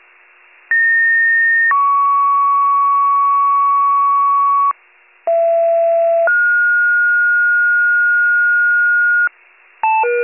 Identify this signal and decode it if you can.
Two-tone page, heard as a one-second tone, then a three-second tone, about 1850 then 1120 Hz, about 670 then 1430 Hz, with noise. It is heard from 0.7 to 9.3 s.